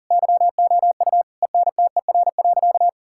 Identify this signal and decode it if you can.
Morse code, heard as a keyed tone, about 40 wpm, 700 Hz. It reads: YOU ENTER.
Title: Sentences from Top 1600 Words 40wpm